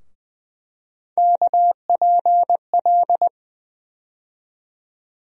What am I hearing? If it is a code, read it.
Morse code, heard as a keyed tone, about 21 wpm, 700 Hz. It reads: KPL
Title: USA Call Sign Postfixes - 3 Letters 20wpm